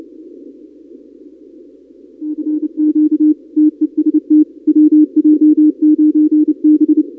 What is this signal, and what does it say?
Morse code, heard as a keyed tone, about 28 wpm, 310 Hz. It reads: CQ TEST WJ9B
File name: contests / 2019-cw-wpx-kt5j-k5pi- -2330 0-31